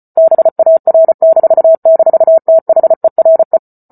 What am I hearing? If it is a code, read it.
Morse code, heard as a keyed tone, about 34 wpm, 650 Hz. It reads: BAR--THERE